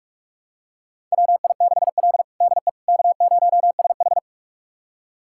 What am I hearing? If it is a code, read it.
Morse code, heard as a keyed tone, about 42 wpm, 700 Hz. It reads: WI6L DE K0SH
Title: USA Callsign DE Callsign - 4 Characters - 45wpm